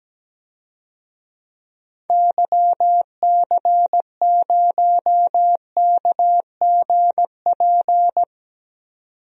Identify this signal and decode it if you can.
Morse code, heard as a keyed tone, about 17 wpm, 700 Hz. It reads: YC0KGP
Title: DXCC-100 Call Signs - 6 Characters - 17wpm